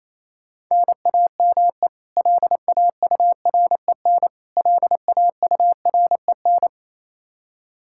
Morse code, heard as a keyed tone, about 27 wpm, 700 Hz. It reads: NAME LAUREN LAUREN
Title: Practice QSO - Exchange 1 - Beginning - 28wpm